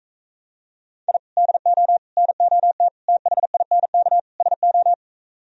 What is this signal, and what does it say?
Morse code, heard as a keyed tone, about 44 wpm, 700 Hz. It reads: I DO NOT THINK SO